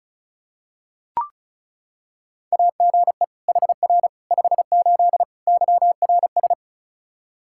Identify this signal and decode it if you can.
Morse code, heard as a keyed tone, about 36 wpm, 700 Hz. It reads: AGE HR 58 YRS
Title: Practice QSO - Exchange 3 - More Info - 35wpm